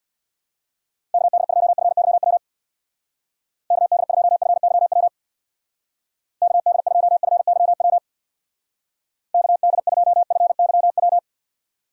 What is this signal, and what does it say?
Morse code, heard as a keyed tone, about 65 wpm, 700 Hz. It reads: KD2FYW KD2FYW KD2FYW KD2FYW